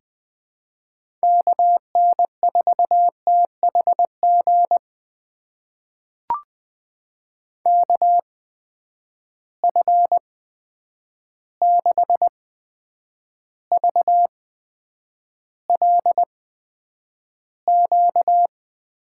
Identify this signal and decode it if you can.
Morse code, heard as a keyed tone, about 20 wpm, spaced as for 9 wpm, 700 Hz. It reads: KN4THG KF6VLQ
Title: USA Call Signs - 6 Characters - 8x Character Spacing - 20wpm